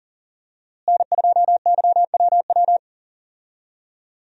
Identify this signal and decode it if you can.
Morse code, heard as a keyed tone, about 40 wpm, 700 Hz. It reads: N2YWW
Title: USA Call Signs - 5 Characters - Repeated 2x - 40wpm